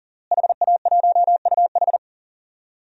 Morse code, heard as a keyed tone, about 40 wpm, 700 Hz. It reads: HA1UH
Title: DXCC-100 Call Signs - 5 Characters - Repeated 3x - 40wpm